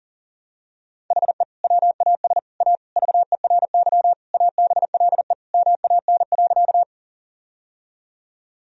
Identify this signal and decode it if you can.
Morse code, heard as a keyed tone, about 40 wpm, 700 Hz. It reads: HE WAS A VERY ABLE MAN.
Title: Sentences from Top 1000 Words - Review Part 2 - 40wpm